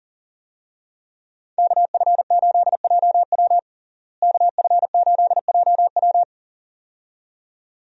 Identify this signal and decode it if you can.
Morse code, heard as a keyed tone, about 40 wpm, 700 Hz. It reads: KF8JW KF8JW
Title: USA Call Signs - 5 Characters - Repeated 2x - 40wpm